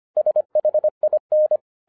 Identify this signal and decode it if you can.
Morse code, heard as a keyed tone, about 25 wpm, 610 Hz. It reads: SHIN